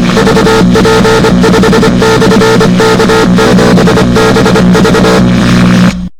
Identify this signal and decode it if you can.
Morse code, heard as a keyed tone, about 25 wpm, 440 Hz, with noise. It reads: VP5/K7BV